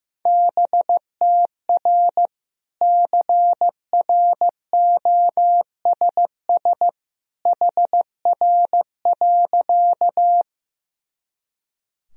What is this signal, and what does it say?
Morse code, heard as a keyed tone, about 14 wpm, 700 Hz. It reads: BTR CROSS HR.
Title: Sentences from Top 400 Words and Abbreviations 15wpm